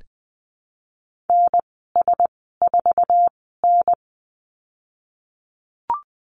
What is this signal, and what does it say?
Morse code, heard as a keyed tone, about 20 wpm, spaced as for 15 wpm, 700 Hz. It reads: NS4N
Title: 048 Lesson 9g - Number 4 - Call Signs 20wpm